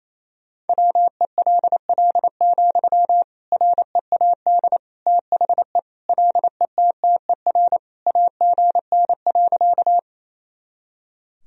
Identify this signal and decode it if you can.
Morse code, heard as a keyed tone, about 28 wpm, 700 Hz. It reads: WELL, READ THE LETTER AGN.
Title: Sentences from Top 300 Words and Abbreviations 28wpm